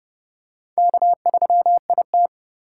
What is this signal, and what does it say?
Morse code, heard as a keyed tone, about 30 wpm, 700 Hz. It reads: K3IT